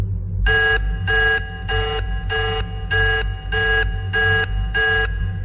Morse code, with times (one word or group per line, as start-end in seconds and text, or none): none